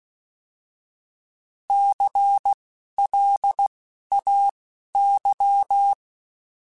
Morse code, heard as a keyed tone, about 16 wpm, spaced as for 11 wpm, 800 Hz. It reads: CLAY
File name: Alphabetical / Alphabetical mp3s / Cs @10